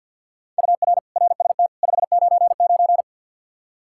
Morse code, heard as a keyed tone, about 51 wpm, 700 Hz. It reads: UR RST 599
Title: Practice QSO - Exchange 1 - Beginning - 50wpm